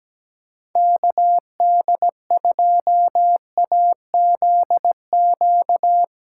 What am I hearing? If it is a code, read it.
Morse code, heard as a keyed tone, about 17 wpm, 700 Hz. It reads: KD2AZQ
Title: USA Call Signs - 6x Character Spacing - 17wpm